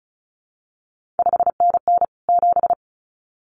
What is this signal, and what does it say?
Morse code, heard as a keyed tone, about 36 wpm, 700 Hz. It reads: 5NN 7
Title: CQ WW DX Contest Exchange Encore - 35wpm